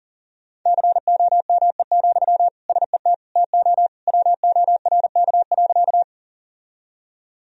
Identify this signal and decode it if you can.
Morse code, heard as a keyed tone, about 40 wpm, 700 Hz. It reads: COME, SET TO WORK.
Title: Sentences from Top 1000 Words - Review Part 1 - 40wpm